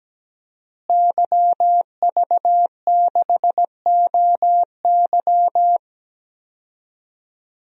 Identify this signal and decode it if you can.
Morse code, heard as a keyed tone, about 17 wpm, 700 Hz. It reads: YV6OY
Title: DXCC-100 Call Signs - 5 Characters - Repeated 2x - 17wpm